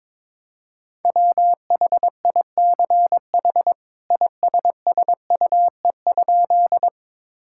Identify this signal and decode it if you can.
Morse code, heard as a keyed tone, about 22 wpm, 700 Hz. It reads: WHICH ISSUE?
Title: Sentences from Top 1000 Words - Review Part 6 - 22wpm